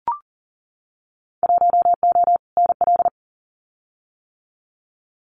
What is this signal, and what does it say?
Morse code, heard as a keyed tone, about 40 wpm, 700 Hz. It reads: 1O NL
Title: Winter Field Day Exchanges Encore 40wpm